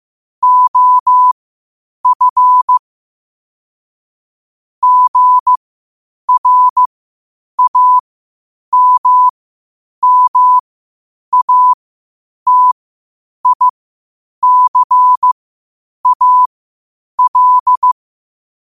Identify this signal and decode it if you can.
Morse code, heard as a keyed tone, about 15 wpm, spaced as for 8 wpm, 1000 Hz. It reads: OF GRAMMATICAL
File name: qotdfiles / 8wpm